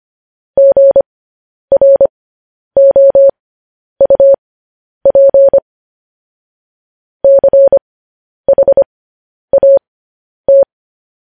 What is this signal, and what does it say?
Morse code, heard as a keyed tone, about 25 wpm, spaced as for 10 wpm, 550 Hz. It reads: GROUP CHAT
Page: news-headlines-in-morse-code-at-10-wpm